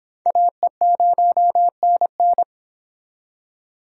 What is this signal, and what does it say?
Morse code, heard as a keyed tone, about 30 wpm, 700 Hz. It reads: AE0NN